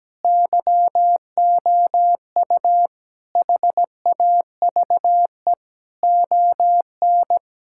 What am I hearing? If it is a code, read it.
Morse code, heard as a keyed tone, about 17 wpm, 700 Hz. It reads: YOU HAVE ON
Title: Sentences from Top 1000 Words - Review Part 5 - 17wpm